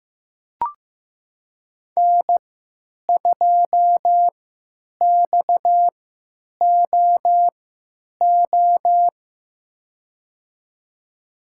Morse code, heard as a keyed tone, about 15 wpm, spaced as for 9 wpm, 700 Hz. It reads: N2XOO
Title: USA Call Signs - 5 Characters - 3x Character Spacing - 15wpm